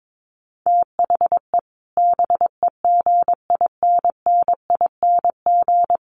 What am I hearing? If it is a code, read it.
Morse code, heard as a keyed tone, about 22 wpm, 700 Hz. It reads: THE BEGINNING
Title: Sets of 3 Words 22wpm